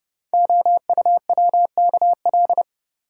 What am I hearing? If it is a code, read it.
Morse code, heard as a keyed tone, about 30 wpm, 700 Hz. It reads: OUWKL
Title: Code Groups - 5 Letters - 30wpm